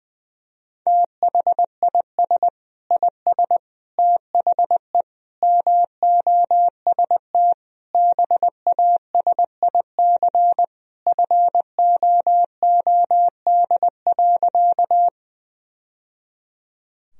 Morse code, heard as a keyed tone, about 21 wpm, 700 Hz. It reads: THIS IS THE MOST BASIC FOOD.
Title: Sentences from Top 1000 Words - Review Part 3 - 20wpm